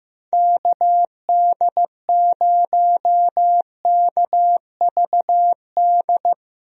Morse code, heard as a keyed tone, about 15 wpm, 700 Hz. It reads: KD0KVD
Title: USA Call Signs - 6 Characters - Repeated 2x - 15wpm